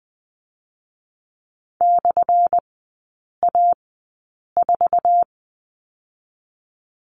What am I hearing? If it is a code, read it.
Morse code, heard as a keyed tone, about 20 wpm, 700 Hz. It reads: / A 4
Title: 218 Lesson 35e - Number 0 - Three Characters 20wpm